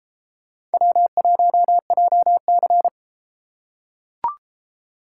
Morse code, heard as a keyed tone, about 31 wpm, 700 Hz. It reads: W1JC